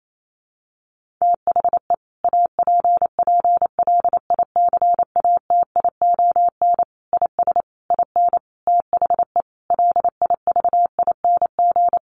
Morse code, heard as a keyed tone, about 28 wpm, 700 Hz. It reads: THE APPLICATION IS IN THE LIVING
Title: Sentences from Top 1200 Words 28wpm